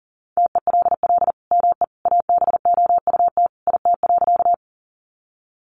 Morse code, heard as a keyed tone, about 40 wpm, 700 Hz. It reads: TELL ME ABOUT IT.
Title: Sentences from Top 200 Words 40wpm